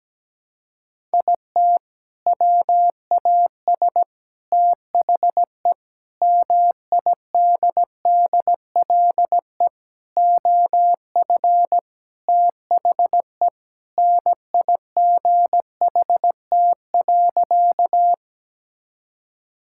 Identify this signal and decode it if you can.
Morse code, heard as a keyed tone, about 17 wpm, 700 Hz. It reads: IT WAS THE MIDDLE OF THE NIGHT.
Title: Sentences from Top 1000 Words - Review Part 6 - 17wpm